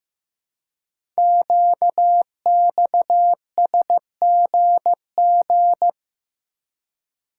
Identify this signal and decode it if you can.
Morse code, heard as a keyed tone, about 15 wpm, 700 Hz. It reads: QXSGG
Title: Code Groups - 5 Letter:Numbers - 15wpm